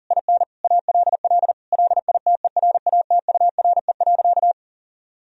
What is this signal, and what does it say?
Morse code, heard as a keyed tone, about 40 wpm, 700 Hz. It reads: IN ALL LITERATURE.
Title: Sentences from Top 1000 Words - Review Part 3 - 40wpm